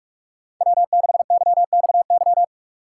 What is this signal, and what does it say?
Morse code, heard as a keyed tone, about 43 wpm, 700 Hz. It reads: W6YXY
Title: USA Call Signs - 5 Characters - 6x Character Spacing - 45wpm